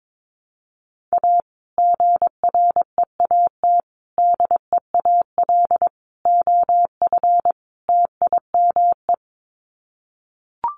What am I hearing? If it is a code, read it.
Morse code, heard as a keyed tone, about 22 wpm, 700 Hz. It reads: A GREAT DEAL OF TIME
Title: Sets of 5 Words Encore2 22wpm